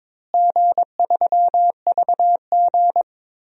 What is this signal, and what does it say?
Morse code, heard as a keyed tone, about 22 wpm, 700 Hz. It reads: G3VG
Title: DXCC-100 Call Signs - 4 Characters - Repeated 3x - 22wpm